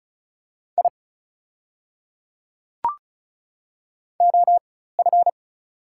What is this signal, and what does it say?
Morse code, heard as a keyed tone, about 34 wpm, spaced as for 16 wpm, 690 Hz. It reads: I OF
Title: 4x Character Spacing - Top 300 Words - 35wpm